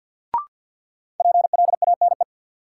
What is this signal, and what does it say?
Morse code, heard as a keyed tone, about 53 wpm, 700 Hz. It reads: PLANE
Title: Sentences from Top 1000 Words - Review Part 3 - New Word Drill - 50wpm